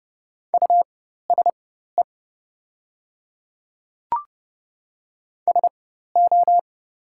Morse code, heard as a keyed tone, about 30 wpm, spaced as for 14 wpm, 700 Hz. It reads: USE SO
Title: 4x Character Spacing - Top 100 Words - 30wpm